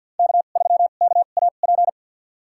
Morse code, heard as a keyed tone, about 46 wpm, spaced as for 34 wpm, 700 Hz. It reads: K3KAP
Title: USA Call Signs - 5 Characters - 2x Character Spacing - 50wpm